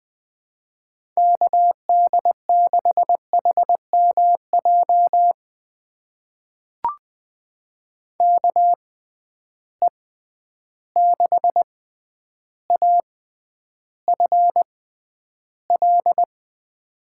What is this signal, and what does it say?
Morse code, heard as a keyed tone, about 20 wpm, spaced as for 10 wpm, 700 Hz. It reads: KD6HMJ KE6AFL
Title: USA Call Signs - 6 Characters - 6x Character Spacing - 20wpm